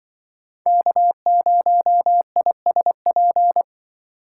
Morse code, heard as a keyed tone, about 25 wpm, 700 Hz. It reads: K0ISP
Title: Speed Racing - USA 1x3 Call Signs 20wpm